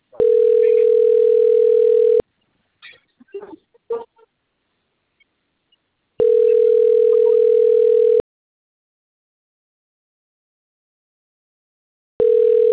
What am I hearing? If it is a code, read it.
Ringback tone (North America), 440 + 480 Hz.